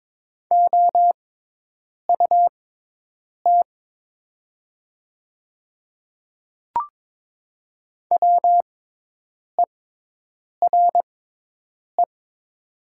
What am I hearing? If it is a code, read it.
Morse code, heard as a keyed tone, about 22 wpm, spaced as for 8 wpm, 700 Hz. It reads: OUT WERE
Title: 6x Character Spacing - Top 100 Words - 22wpm